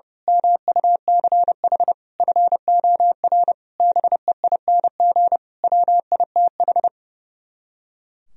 Morse code, heard as a keyed tone, about 28 wpm, 700 Hz. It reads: MUCH FOR BEING WITH
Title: Speed-Racing - Sets of 5 Words 30wpm